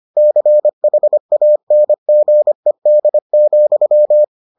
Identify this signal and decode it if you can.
Morse code, heard as a keyed tone, about 25 wpm, 600 Hz. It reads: CHANGED,